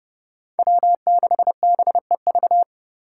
Morse code, heard as a keyed tone, about 29 wpm, 700 Hz. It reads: W6BEV